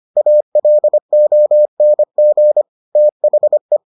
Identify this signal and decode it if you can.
Morse code, heard as a keyed tone, about 25 wpm, 600 Hz. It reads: ALONG THE